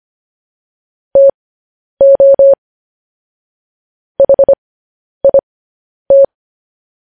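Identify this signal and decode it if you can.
Morse code, heard as a keyed tone, about 25 wpm, spaced as for 10 wpm, 550 Hz. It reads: TO HIT
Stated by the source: Morse code news headlines broadcast at 10 wpm.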